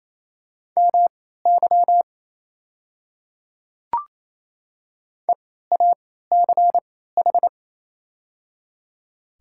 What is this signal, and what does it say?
Morse code, heard as a keyed tone, about 28 wpm, spaced as for 16 wpm, 690 Hz. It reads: MY EACH
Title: 3x Character Spacing - Top 100 Words - 28wpm